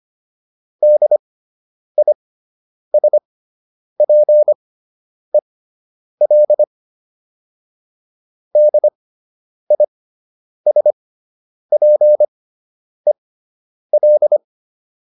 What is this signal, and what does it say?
Morse code, heard as a keyed tone, about 25 wpm, spaced as for 9 wpm, 600 Hz. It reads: DISPEL DISPEL